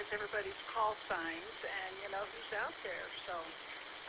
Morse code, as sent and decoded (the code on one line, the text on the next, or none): none